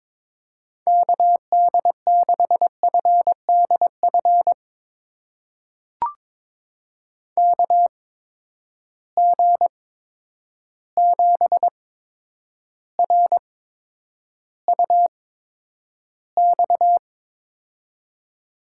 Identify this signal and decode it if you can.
Morse code, heard as a keyed tone, about 21 wpm, spaced as for 9 wpm, 700 Hz. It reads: KD6FDF KG7RUX
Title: USA Call Signs - 6 Characters - 8x Character Spacing - 22wpm